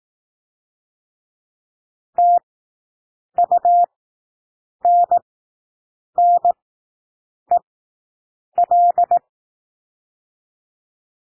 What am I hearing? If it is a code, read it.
Morse code, heard as a keyed tone, about 18 wpm, spaced as for 7 wpm, 700 Hz. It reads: TUNNEL
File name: SOUND WORDS 18 wpm